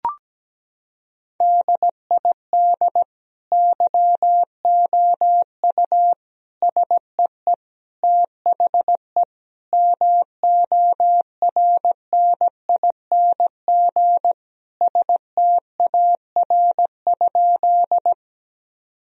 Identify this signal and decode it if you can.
Morse code, heard as a keyed tone, about 17 wpm, 700 Hz. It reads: DID YOU SEE THE MORNING STAR?
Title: Sentences from Top 1000 Words - Review Part 2 - 17wpm